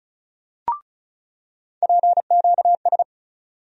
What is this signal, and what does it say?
Morse code, heard as a keyed tone, about 36 wpm, 700 Hz. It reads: PQS
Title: USA Call Sign Postfixes - 3 Letters 35wpm